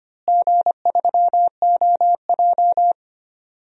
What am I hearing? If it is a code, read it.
Morse code, heard as a keyed tone, about 25 wpm, 700 Hz. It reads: G3OJ
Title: DXCC-100 Call Signs - 4 Characters - Repeated 2x - 25wpm